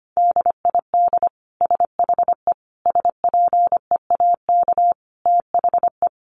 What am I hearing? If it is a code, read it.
Morse code, heard as a keyed tone, about 25 wpm, 700 Hz. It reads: DID SHE SPEAK THE
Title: Sentences from Top 1100 Words 25wpm